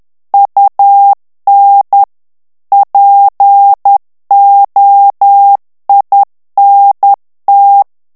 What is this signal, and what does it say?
Morse code, heard as a keyed tone, about 10 wpm, 800 Hz. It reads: UN POINT
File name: morse mp3 / FRANCAIS TEXTE CLAIR 2 VITESSE 10 MOTS MINUTES